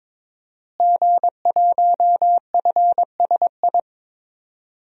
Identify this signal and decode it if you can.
Morse code, heard as a keyed tone, about 22 wpm, 700 Hz. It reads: G1FSI
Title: DXCC-100 Call Signs - 5 Characters - Repeated 2x - 22wpm